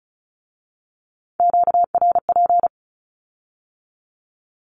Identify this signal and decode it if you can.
Morse code, heard as a keyed tone, about 36 wpm, 700 Hz. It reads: QRP
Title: Q-Codes with Spoken Meaning - 35wpm